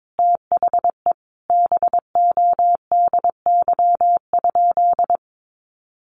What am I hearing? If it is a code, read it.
Morse code, heard as a keyed tone, about 22 wpm, 700 Hz. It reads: THE BODY?
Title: Sentences from Top 400 Words 22wpm